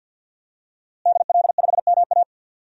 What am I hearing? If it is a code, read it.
Morse code, heard as a keyed tone, about 50 wpm, 700 Hz. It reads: DL5MA
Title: DXCC-100 Call Signs - 5 Characters - Repeated 3x - 50wpm